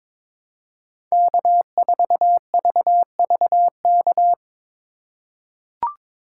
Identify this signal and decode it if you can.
Morse code, heard as a keyed tone, about 22 wpm, 700 Hz. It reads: K4VVK
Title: USA Call Signs - 5 Characters - 4x Character Spacing - 22wpm